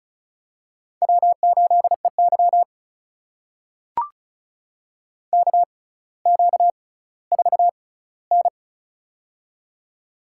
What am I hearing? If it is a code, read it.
Morse code, heard as a keyed tone, about 36 wpm, spaced as for 18 wpm, 700 Hz. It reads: W8EY KQ4N
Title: USA Call Signs - 4 Characters - 6x Character Spacing - 35wpm